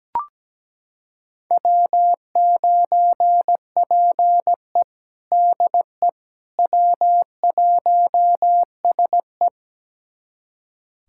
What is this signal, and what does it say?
Morse code, heard as a keyed tone, about 17 wpm, 700 Hz. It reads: W9PE DE W1SE
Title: USA Callsign DE Callsign - 4 Characters - 17wpm